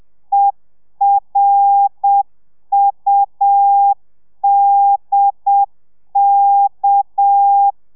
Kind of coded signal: Morse code